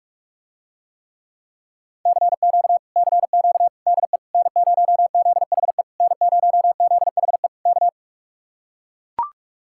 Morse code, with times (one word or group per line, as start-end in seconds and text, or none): 2.1-2.8 CQ
3.0-3.7 CQ
3.9-4.2 DE
4.3-5.8 N0ZHE
6.0-7.5 N0ZHE
7.7-7.9 K